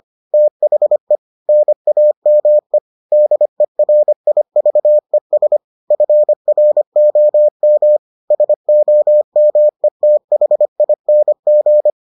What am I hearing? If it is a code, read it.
Morse code, heard as a keyed tone, about 25 wpm, 600 Hz. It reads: THE NAME DERIVES FROM SOMETHING